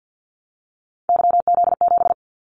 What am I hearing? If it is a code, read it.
Morse code, heard as a keyed tone, about 50 wpm, 700 Hz. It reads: YZ7
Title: Callsign Prefixes for Top 25 DXCC - 50wpm